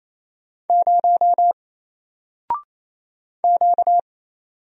Morse code, heard as a keyed tone, about 27 wpm, 700 Hz. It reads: 0 Q
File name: Single-Letter-Number-28wpm